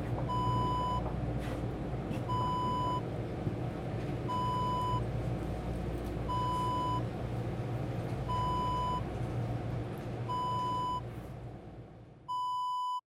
There is no code.